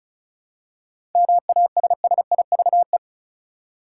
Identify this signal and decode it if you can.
Morse code, heard as a keyed tone, about 36 wpm, 700 Hz. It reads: MASSIVE
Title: Seven Letter Words - 35wpm